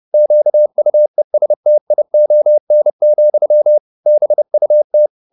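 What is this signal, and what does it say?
Morse code, heard as a keyed tone, about 30 wpm, 600 Hz. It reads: QUESTION, BUT
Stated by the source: Morse code at 30 wpm, the novel Never by Ken Follett in English, from CW med Gnister.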